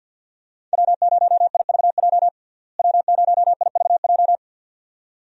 Morse code, heard as a keyed tone, about 48 wpm, 700 Hz. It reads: W0IVJ W0IVJ
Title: USA Call Signs - 5 Characters - Repeated 2x - 50wpm